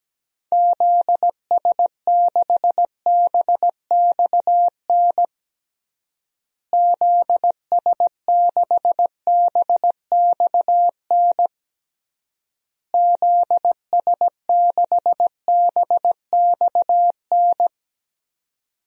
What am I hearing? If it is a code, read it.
Morse code, heard as a keyed tone, about 17 wpm, 700 Hz. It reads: ZS6BXN ZS6BXN ZS6BXN
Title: DXCC-100 Call Signs - 6 Characters - Repeated 3x - 17wpm